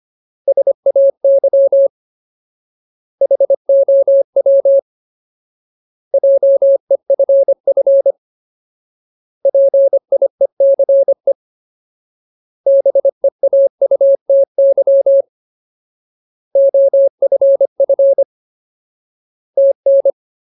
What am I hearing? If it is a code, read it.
Morse code, heard as a keyed tone, about 25 wpm, 550 Hz. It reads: SAY HOW JEFF PIECE BEAUTY OFF TN
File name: Ultimate Word List / 25WPM Ultimate